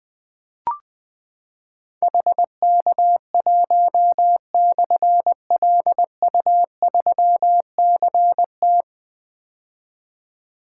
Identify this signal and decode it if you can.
Morse code, heard as a keyed tone, about 20 wpm, 700 Hz. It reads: HK1/LU3CT
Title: Difficult Call Signs 20wpm